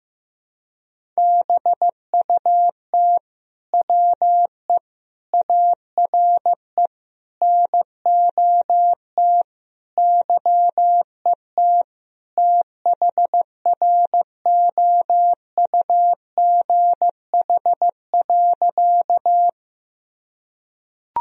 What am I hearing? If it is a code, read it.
Morse code, heard as a keyed tone, about 15 wpm, 700 Hz. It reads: BUT WE ARE NOT YET THROUGH.
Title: Sentences from Top 1000 Words - Review Part 1 - 15wpm